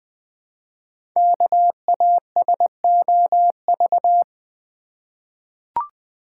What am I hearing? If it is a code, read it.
Morse code, heard as a keyed tone, about 20 wpm, 700 Hz. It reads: KASOV
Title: Code Groups - 5 Letters - 20wpm